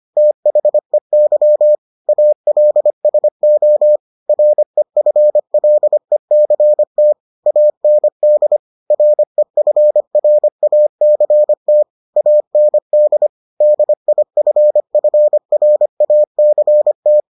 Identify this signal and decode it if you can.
Morse code, heard as a keyed tone, about 25 wpm, 600 Hz. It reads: THEY ALSO REFLECT AND REFRACT AND DIFFRACT